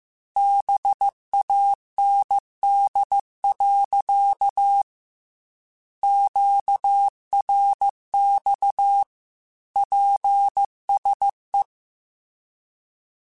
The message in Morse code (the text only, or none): BAND. QRX PSE